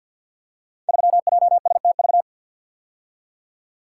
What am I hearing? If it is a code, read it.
Morse code, heard as a keyed tone, about 46 wpm, 700 Hz. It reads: 3JSTV
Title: Code Groups - 5 Letter:Numbers - 50wpm